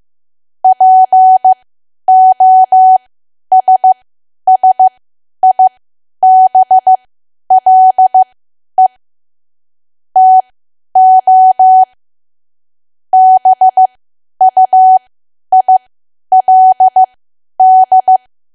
Morse code, heard as a keyed tone, about 15 wpm, spaced as for 10 wpm, 750 Hz. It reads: POSSIBLE TO BUILD
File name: cwa / W1AW-Level2 / 10WPM